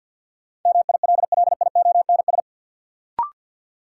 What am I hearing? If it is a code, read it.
Morse code, heard as a keyed tone, about 50 wpm, 700 Hz. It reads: MILLIONS
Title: Sets of 5 Words - New Word Drill - 50wpm